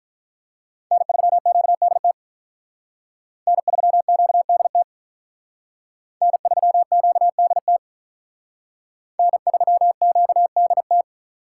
Morse code, heard as a keyed tone, about 45 wpm, 700 Hz. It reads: N3QDT N3QDT N3QDT N3QDT